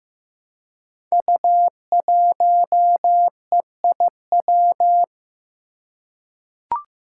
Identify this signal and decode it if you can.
Morse code, heard as a keyed tone, about 15 wpm, 700 Hz. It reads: U1EIW
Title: Five Challenging Letter-Number Combinations 15wpm